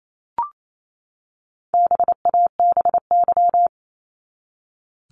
Morse code, heard as a keyed tone, about 28 wpm, 700 Hz. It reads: BABY
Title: Top 1001-1250 Words 28wpm